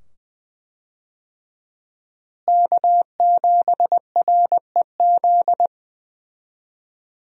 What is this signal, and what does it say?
Morse code, heard as a keyed tone, about 20 wpm, 700 Hz. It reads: K7REZ
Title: USA Call Signs - 5 Characters - Repeated 2x - 20wpm